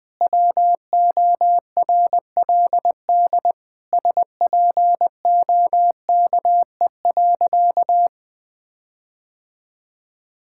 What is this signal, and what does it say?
Morse code, heard as a keyed tone, about 20 wpm, 700 Hz. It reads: WORLD SPOKE.
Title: Sentences with 6-Word Phrases - 20wpm